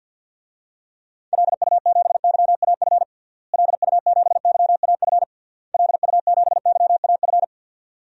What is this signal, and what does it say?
Morse code, heard as a keyed tone, about 47 wpm, 700 Hz. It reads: LU7YAF LU7YAF LU7YAF
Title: DXCC-100 Call Signs - 6 Characters - Repeated 3x - 50wpm